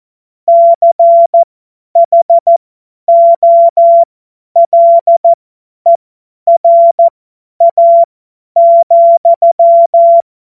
Morse code, heard as a keyed tone, about 14 wpm, spaced as for 10 wpm, 680 Hz. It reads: CHOLERA,
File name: Morsekurs / Witze / Witz 26 März 2024 50 F70